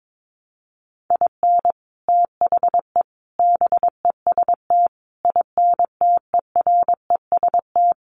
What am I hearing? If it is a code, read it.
Morse code, heard as a keyed tone, about 22 wpm, 700 Hz. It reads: IN THE BEST INTEREST
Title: Sets of 5 Words Encore 22wpm